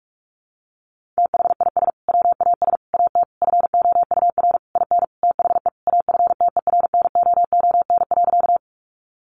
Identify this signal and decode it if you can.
Morse code, heard as a keyed tone, about 44 wpm, 700 Hz. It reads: THIS WAS AT FOUR IN THE AFTERNOON.
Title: Sentences from Top 1500 Words 45wpm